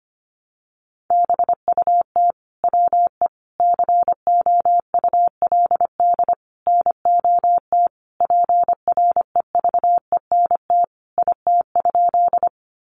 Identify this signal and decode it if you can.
Morse code, heard as a keyed tone, about 25 wpm, 700 Hz. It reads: BUT WE COULD NOT PREVENT IT?
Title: Sentences from Top 1100 Words 25wpm